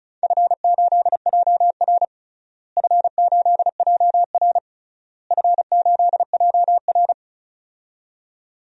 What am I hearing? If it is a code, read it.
Morse code, heard as a keyed tone, about 34 wpm, 700 Hz. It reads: F8JR F8JR F8JR
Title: DXCC-100 Call Signs - 4 Characters - Repeated 3x - 35wpm